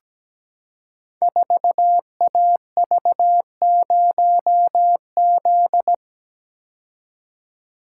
Morse code, heard as a keyed tone, about 17 wpm, 690 Hz. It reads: 4AV0Z
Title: Code Groups - 5 Letter:Numbers - 17wpm